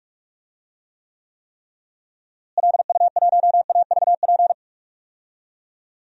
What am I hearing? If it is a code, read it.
Morse code, heard as a keyed tone, about 44 wpm, 700 Hz. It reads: LU1AVP
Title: DXCC-100 Call Signs - 6 Characters - Repeated 3x - 45wpm